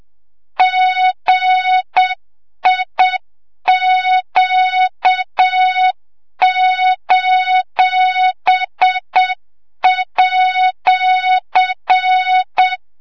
Morse code, heard as a keyed tone, about 7 wpm, 3700 Hz, with noise. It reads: GIQ:@
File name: Morze-HA2MN / hang / 25 GYAKORLAT